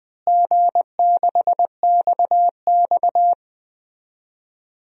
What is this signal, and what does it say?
Morse code, heard as a keyed tone, about 20 wpm, 700 Hz. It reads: G6XX
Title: DXCC-100 Call Signs - 4 Characters - Repeated 2x - 20wpm